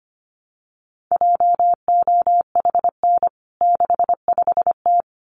Morse code, heard as a keyed tone, about 25 wpm, 700 Hz. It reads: JOHN 65T
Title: CWT Exchange 25wpm